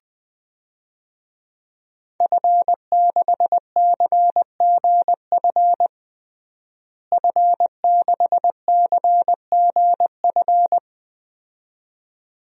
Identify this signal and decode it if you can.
Morse code, heard as a keyed tone, about 21 wpm, 700 Hz. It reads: F6CGF F6CGF
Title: DXCC-100 Call Signs - 5 Characters - Repeated 2x - 20wpm